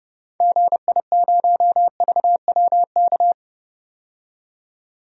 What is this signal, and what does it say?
Morse code, heard as a keyed tone, about 30 wpm, 700 Hz. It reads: GI0VWK
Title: DXCC-100 Call Signs - 6 Characters - 30wpm